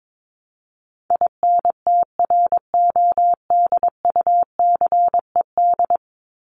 Morse code, heard as a keyed tone, about 22 wpm, 700 Hz. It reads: INTRODUCED
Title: ED Suffix - Top 100 Words with Base Words 22wpm